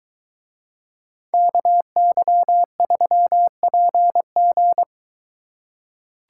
Morse code, heard as a keyed tone, about 22 wpm, 700 Hz. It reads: KY3PG